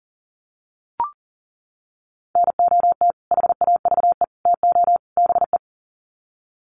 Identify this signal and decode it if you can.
Morse code, heard as a keyed tone, about 40 wpm, 700 Hz. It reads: NOT HAVE TO BE
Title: Sets of 4 Words 40wpm